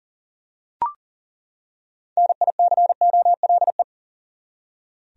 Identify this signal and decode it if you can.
Morse code, heard as a keyed tone, about 40 wpm, 700 Hz. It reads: NICOLE
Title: USA Common Names - 40wpm